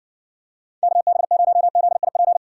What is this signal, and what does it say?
Morse code, heard as a keyed tone, about 56 wpm, 700 Hz. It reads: KB0ZIP